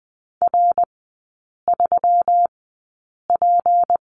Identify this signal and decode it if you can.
Morse code, heard as a keyed tone, about 20 wpm, spaced as for 8 wpm, 690 Hz. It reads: R3P
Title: 159 Lesson 26e - Letter g - Three Characters 20wpm